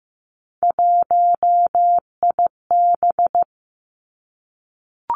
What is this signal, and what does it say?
Morse code, heard as a keyed tone, about 15 wpm, 700 Hz. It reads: 1IB
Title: Three Challenging Letter-Number Combinations 15wpm